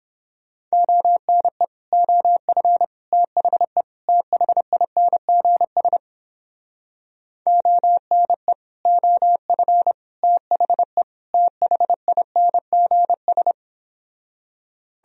Morse code, heard as a keyed tone, about 31 wpm, 700 Hz. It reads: ONE OF THE THINGS ONE OF THE THINGS